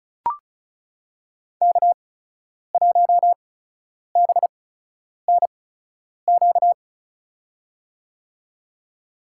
Morse code, heard as a keyed tone, about 36 wpm, spaced as for 10 wpm, 700 Hz. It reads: K1BNQ
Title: USA Call Signs - 5 Characters - 8x Character Spacing - 35wpm